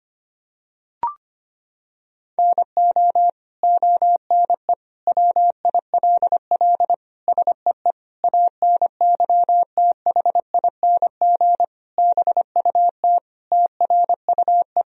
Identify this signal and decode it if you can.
Morse code, heard as a keyed tone, about 25 wpm, 700 Hz. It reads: NO ONE WILL SEE ANYTHING BUT TRUE
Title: Sentences from Top 1000 Words - Review Part 3 - 25wpm